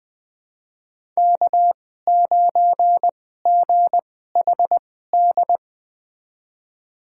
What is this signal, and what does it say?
Morse code, heard as a keyed tone, about 20 wpm, spaced as for 14 wpm, 700 Hz. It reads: K9GHD
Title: USA Call Signs - 2x Character Spacing - 20wpm